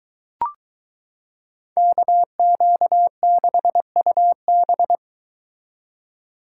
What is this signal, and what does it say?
Morse code, heard as a keyed tone, about 23 wpm, 700 Hz. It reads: KQ6UB